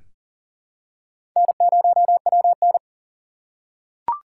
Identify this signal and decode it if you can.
Morse code, heard as a keyed tone, about 39 wpm, 700 Hz. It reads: N0WN
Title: USA Call Signs - 4 Characters - 2x Character Spacing - 40wpm